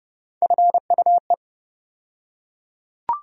Morse code, heard as a keyed tone, about 30 wpm, 700 Hz. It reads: FUE
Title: USA Call Sign Postfixes - 3 Letters 30wpm